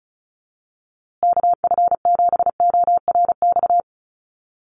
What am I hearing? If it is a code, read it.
Morse code, heard as a keyed tone, about 34 wpm, 700 Hz. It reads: KF7ORX
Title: USA Call Signs Repeated 3x 35wpm